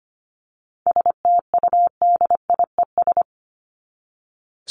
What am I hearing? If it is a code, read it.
Morse code, heard as a keyed tone, about 25 wpm, 700 Hz. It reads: STUDIES
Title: ES:S Suffix - Top 100 Words with Base Words 25wpm